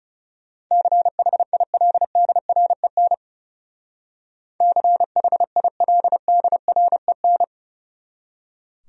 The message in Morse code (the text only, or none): CHILDREN CHILDREN